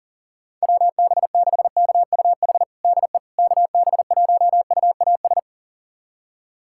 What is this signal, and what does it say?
Morse code, heard as a keyed tone, about 40 wpm, 700 Hz. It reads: WB6KUH DE KB1UAS
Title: USA Callsign DE Callsign - 6 Characters - 40wpm